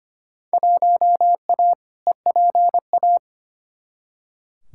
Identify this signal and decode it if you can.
Morse code, heard as a keyed tone, about 25 wpm, 700 Hz. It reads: 1A EPA
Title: Field Day Exchanges - 25wpm